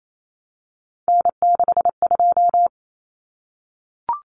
Morse code, heard as a keyed tone, about 28 wpm, 700 Hz. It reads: N62
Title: Common Cut Numbers - 3-Digit 28wpm